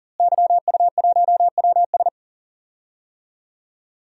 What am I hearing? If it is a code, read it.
Morse code, heard as a keyed tone, about 40 wpm, 700 Hz. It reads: YU1WS
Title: DXCC-100 Call Signs - 5 Characters - Repeated 2x - 40wpm